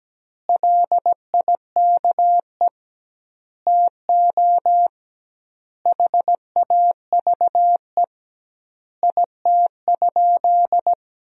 Morse code, heard as a keyed tone, about 17 wpm, 700 Hz. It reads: LIKE TO HAVE IT?
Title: 2x Word-Spacing - Sentences from Top 100 Words - 17wpm